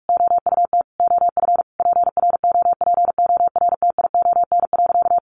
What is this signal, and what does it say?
Morse code, heard as a keyed tone, about 44 wpm, 700 Hz. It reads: OUT OF PROPORTION.
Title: A Sound Of Thunder 45wpm